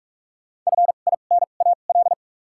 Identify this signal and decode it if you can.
Morse code, heard as a keyed tone, about 46 wpm, spaced as for 33 wpm, 700 Hz. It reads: FINAL
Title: 2x Character Spacing - Top 500 Words - 45wpm